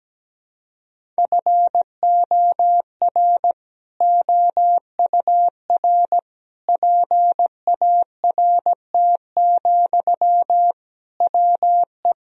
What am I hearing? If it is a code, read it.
Morse code, heard as a keyed tone, about 17 wpm, 700 Hz. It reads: FOR OUR PART, WE
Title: Sentences from Top 1000 Words - Review Part 1 - 17wpm